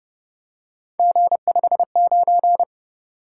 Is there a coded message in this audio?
Yes: Morse code from 1.0 to 2.6 s, heard as a keyed tone, about 29 wpm, 700 Hz. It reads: G59